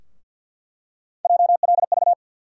Morse code, heard as a keyed tone, about 47 wpm, 700 Hz. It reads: JLV
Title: USA Call Sign Postfixes - 3 Letters 50wpm